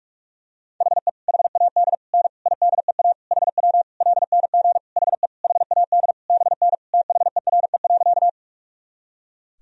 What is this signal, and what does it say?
Morse code, heard as a keyed tone, about 46 wpm, 700 Hz. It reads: HE HAD N IDEA HW LNG HE HAD BN THERE.
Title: Sentences from Top 300 Words and Abbreviations 45wpm